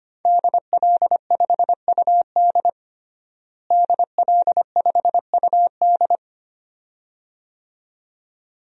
Morse code, heard as a keyed tone, about 25 wpm, 700 Hz. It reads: DL5UD DL5UD
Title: DXCC-100 Call Signs - 5 Characters - Repeated 2x - 25wpm